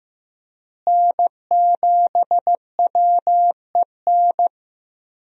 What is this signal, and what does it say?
Morse code, heard as a keyed tone, about 15 wpm, 700 Hz. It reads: N7WEN
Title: USA Call Signs - 5 Characters - Repeated 3x - 15wpm